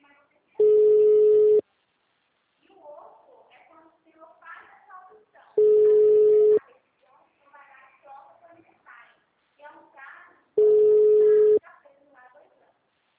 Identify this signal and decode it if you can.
Ringback tone (Europe, 425 Hz), with speech.